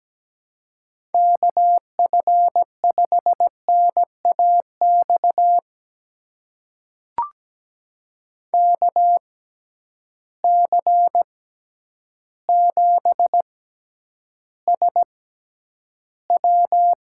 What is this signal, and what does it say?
Morse code, heard as a keyed tone, about 17 wpm, spaced as for 9 wpm, 700 Hz. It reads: KF5NAX KC7SW
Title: USA Call Signs - 6 Characters - 6x Character Spacing - 17wpm